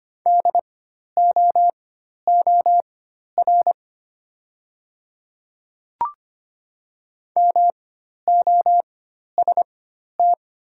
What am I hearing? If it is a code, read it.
Morse code, heard as a keyed tone, about 25 wpm, spaced as for 12 wpm, 700 Hz. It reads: DOOR MOST
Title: 4x Character Spacing - Top 300 Words - 25wpm